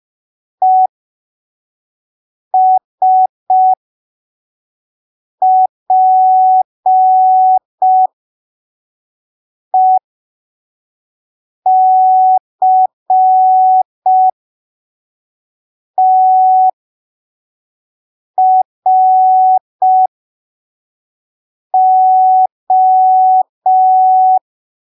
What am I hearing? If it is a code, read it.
Morse code, heard as a keyed tone, about 5 wpm, spaced as for 3 wpm, 750 Hz. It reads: ESPECTRO
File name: arquivos / texto / texto 06 05ppm